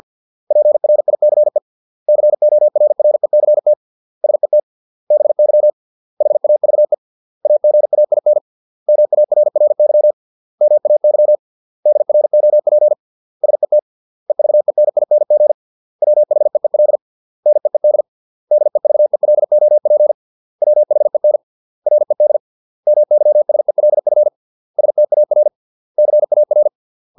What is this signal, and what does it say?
Morse code, heard as a keyed tone, about 50 wpm, 600 Hz. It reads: PRICE CORRECT SET BY HAVE AGAIN MARRY MAY DROP SET EVENING WHEEL NEED DEVELOP WHEN RED MYSELF STAR CAR